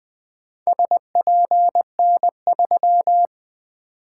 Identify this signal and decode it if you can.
Morse code, heard as a keyed tone, about 20 wpm, 700 Hz. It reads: SPN3
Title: Four Letter-Number Combinations - 20wpm